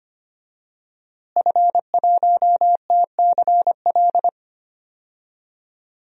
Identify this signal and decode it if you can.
Morse code, heard as a keyed tone, about 25 wpm, 700 Hz. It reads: F1TCL
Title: DXCC-100 Call Signs - 5 Characters - Repeated 3x - 25wpm